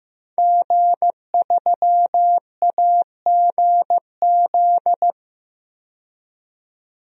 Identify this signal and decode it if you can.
Morse code, heard as a keyed tone, about 15 wpm, 700 Hz. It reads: G3AGZ
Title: DXCC-100 Call Signs - 5 Characters - 15wpm